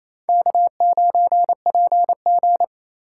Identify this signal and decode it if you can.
Morse code, heard as a keyed tone, about 35 wpm, 700 Hz. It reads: K9PG